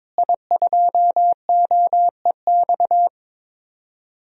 Morse code, heard as a keyed tone, about 22 wpm, 700 Hz. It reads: I2OEX
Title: DXCC-100 Call Signs - 5 Characters - Repeated 2x - 22wpm